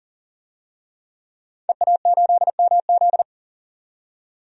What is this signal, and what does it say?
Morse code, heard as a keyed tone, about 38 wpm, 700 Hz. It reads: EA8MZ
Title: DXCC-100 Call Signs - 5 Characters - 40wpm